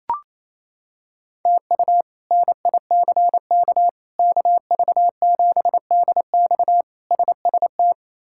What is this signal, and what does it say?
Morse code, heard as a keyed tone, about 28 wpm, 700 Hz. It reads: TU NICK KV7DX SST
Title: K1USN SST QSO Practice - 28wpm